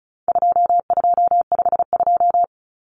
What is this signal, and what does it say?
Morse code, heard as a keyed tone, about 34 wpm, 700 Hz. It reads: 2252
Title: Common Cut Numbers - 4-Digit 35wpm